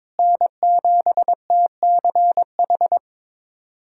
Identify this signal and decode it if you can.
Morse code, heard as a keyed tone, about 22 wpm, 700 Hz. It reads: N7TCH
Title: USA Callsign DE Callsign - 5 Characters - 22wpm